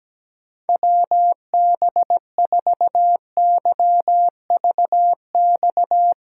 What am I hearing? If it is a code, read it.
Morse code, heard as a keyed tone, about 17 wpm, 700 Hz. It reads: WB4YVX
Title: USA Callsign DE Callsign - 6 Characters - 17wpm